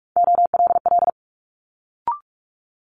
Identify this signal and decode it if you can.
Morse code, heard as a keyed tone, about 46 wpm, 700 Hz. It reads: OLL
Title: Multi-letter Phonemes Quick Fire 45wpm